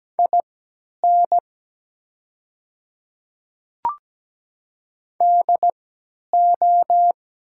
Morse code, heard as a keyed tone, about 17 wpm, spaced as for 10 wpm, 700 Hz. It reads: IN DO
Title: 3x Character Spacing - Top 500 Words - 17wpm